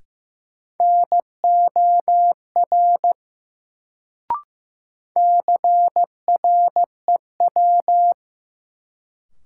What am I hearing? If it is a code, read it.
Morse code, heard as a keyed tone, about 15 wpm, 700 Hz. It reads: NOR CREW
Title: Sentences from Top 1500 Words - New Word Drill - 15wpm